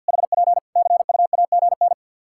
Morse code, heard as a keyed tone, about 53 wpm, 700 Hz. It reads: HP CUAGN
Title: Practice QSO - Exchange 4 - Ending - 50wpm